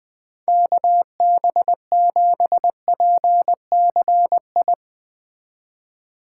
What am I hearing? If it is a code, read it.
Morse code, heard as a keyed tone, about 20 wpm, 700 Hz. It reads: KB7PCI
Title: Speed Racing - USA Call Signs 17wpm